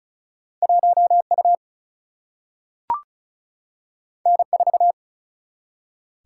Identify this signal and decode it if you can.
Morse code, heard as a keyed tone, about 33 wpm, 700 Hz. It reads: 1U N4
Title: Single-Letter Single-Number 35wpm